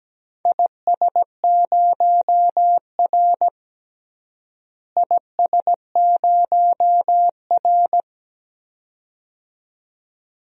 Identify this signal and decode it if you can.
Morse code, heard as a keyed tone, about 17 wpm, 700 Hz. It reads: IS0R IS0R
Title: DXCC-100 Call Signs - 4 Characters - Repeated 2x - 17wpm